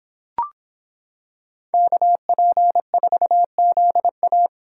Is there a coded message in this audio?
Yes: Morse code from 1.7 to 4.5 s, heard as a keyed tone, about 26 wpm, 700 Hz. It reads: KP4ZA